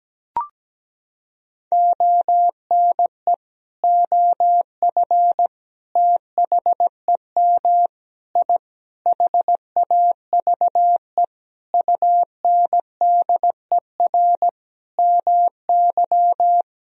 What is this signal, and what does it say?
Morse code, heard as a keyed tone, about 17 wpm, 700 Hz. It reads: ONE OF THEM I HAVE UNDER MY
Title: Sentences from Top 1000 Words - Review Part 1 - 17wpm